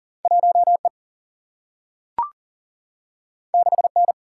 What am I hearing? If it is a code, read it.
Morse code, heard as a keyed tone, about 39 wpm, 700 Hz. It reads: 1E 6N
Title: Single-Letter Single-Number 40wpm